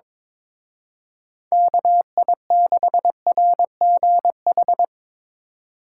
Morse code, heard as a keyed tone, about 24 wpm, 700 Hz. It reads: KI6RGH
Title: Speed Racing - USA 2x3 Call Signs 22wpm